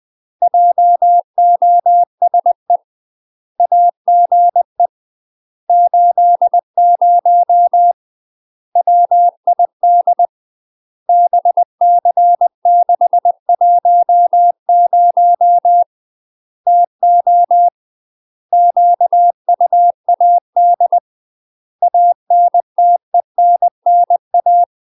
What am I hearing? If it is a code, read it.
Morse code, heard as a keyed tone, about 20 wpm, 700 Hz. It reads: JOSE AGE 80 WID BC610 TO QUAD ANTENNA
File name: UK MASTER QSOs TRAINING / 20WPM / UK MASTER QSOs 0032